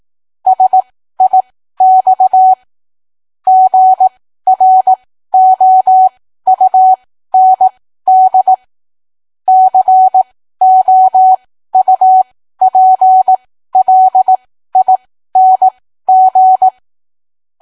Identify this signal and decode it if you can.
Morse code, heard as a keyed tone, about 18 wpm, spaced as for 13 wpm, 750 Hz. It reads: SIX GROUND COUPLING